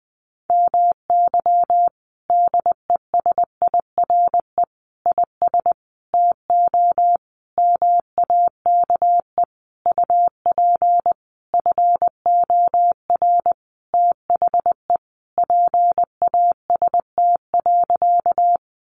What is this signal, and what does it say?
Morse code, heard as a keyed tone, about 20 wpm, 700 Hz. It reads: MY DESIRE IS TO MAKE UP FOR THE PAST.
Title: Sentences from Top 1100 Words 20wpm